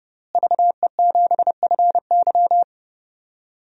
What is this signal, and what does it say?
Morse code, heard as a keyed tone, about 32 wpm, 700 Hz. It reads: VE7FY
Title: DXCC-100 Call Signs - 5 Characters - Repeated 3x - 30wpm